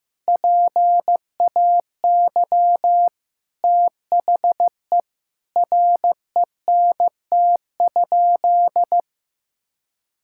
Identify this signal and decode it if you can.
Morse code, heard as a keyed tone, about 15 wpm, 700 Hz. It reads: PAY THE RENT?
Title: Sentences from Top 1000 Words - Review Part 5 - 15wpm